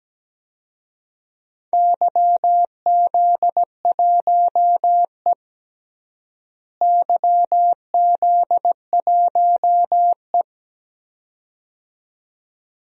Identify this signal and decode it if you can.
Morse code, heard as a keyed tone, about 17 wpm, 700 Hz. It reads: YZ1E YZ1E
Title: DXCC-100 Call Signs - 4 Characters - Repeated 2x - 17wpm